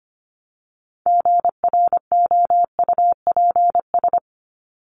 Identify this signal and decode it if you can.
Morse code, heard as a keyed tone, about 25 wpm, 700 Hz. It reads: GROUPS
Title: ES:S Suffix - Top 100 Words with Base Words 25wpm